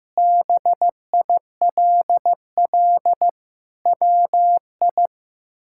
Morse code, heard as a keyed tone, about 15 wpm, 700 Hz. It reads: BILL WI
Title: K1USN SST QSO Practice - 15wpm